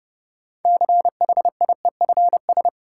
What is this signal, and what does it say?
Morse code, heard as a keyed tone, about 30 wpm, 700 Hz. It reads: CHIEFS
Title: Sets of 5 Words Encore - New Word Drill - 30wpm